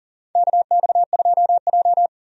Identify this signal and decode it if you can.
Morse code, heard as a keyed tone, about 40 wpm, 700 Hz. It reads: KX2J
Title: USA Call Signs - 4 Characters - 40wpm